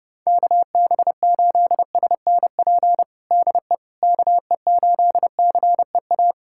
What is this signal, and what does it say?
Morse code, heard as a keyed tone, about 32 wpm, 700 Hz. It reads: KB8SNP DE KE8CEA
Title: USA Callsign DE Callsign - 6 Characters - 30wpm